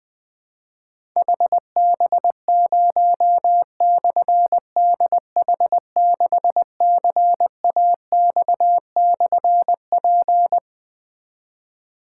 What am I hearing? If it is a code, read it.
Morse code, heard as a keyed tone, about 21 wpm, 700 Hz. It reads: HB0/DH6CAX/P
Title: Difficult Call Signs 20wpm